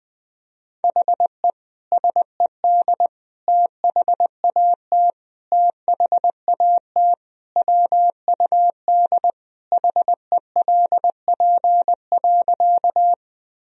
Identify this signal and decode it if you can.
Morse code, heard as a keyed tone, about 20 wpm, 700 Hz. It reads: HE SED THAT THAT WUD HELP.
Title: Sentences from Top 200 Words and Abbreviations 20wpm